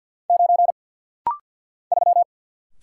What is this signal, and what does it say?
Morse code, heard as a keyed tone, about 48 wpm, 700 Hz. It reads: 9 3
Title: Single Letter-Number - Rapid-Fire 50wpm